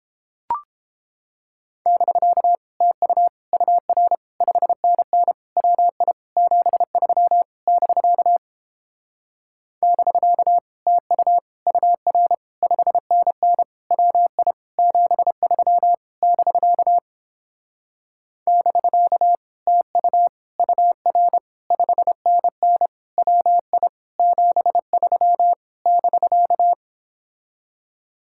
Morse code, heard as a keyed tone, about 29 wpm, 700 Hz. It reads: <BK> TU UR 5NN WI 73 <BK> <BK> TU UR 5NN WI 73 <BK> <BK> TU UR 5NN WI 73 <BK>